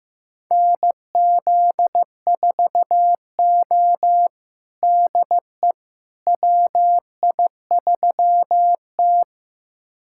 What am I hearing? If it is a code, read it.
Morse code, heard as a keyed tone, about 15 wpm, 700 Hz. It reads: NZ4O DE WI3T
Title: USA Callsign DE Callsign - 4 Characters - 15wpm